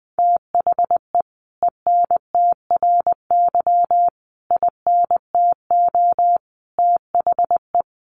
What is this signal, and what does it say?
Morse code, heard as a keyed tone, about 20 wpm, 700 Hz. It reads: THE ENTRY INTO THE
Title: Sentences from Top 1500 Words 20wpm